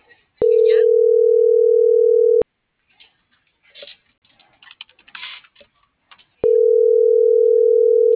Ringback tone (North America), 440 + 480 Hz.